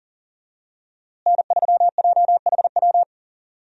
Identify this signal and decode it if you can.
Morse code, heard as a keyed tone, about 37 wpm, 700 Hz. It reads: N3JHW
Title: USA Call Signs - 5 Characters - Repeated 2x - 40wpm